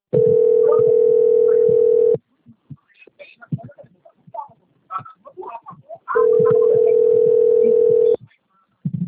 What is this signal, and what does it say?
Ringback tone (North America), 440 + 480 Hz, with speech.
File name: out- - - - - .57604